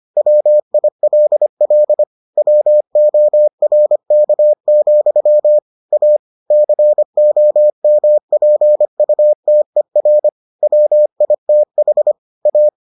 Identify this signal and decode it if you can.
Morse code, heard as a keyed tone, about 25 wpm, 600 Hz. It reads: WILL WORK, A COMPUTER WITH A